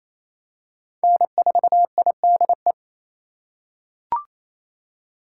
Morse code, heard as a keyed tone, about 28 wpm, 700 Hz. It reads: N4IDE
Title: USA Call Signs - 8x Character Spacing - 28wpm